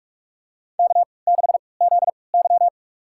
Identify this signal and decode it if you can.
Morse code, heard as a keyed tone, about 46 wpm, spaced as for 26 wpm, 700 Hz. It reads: K6ZY
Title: USA Call Signs - 4 Characters - 3x Character Spacing - 45wpm